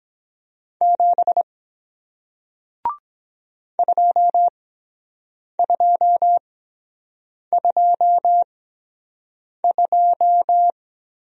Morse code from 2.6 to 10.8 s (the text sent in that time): 2222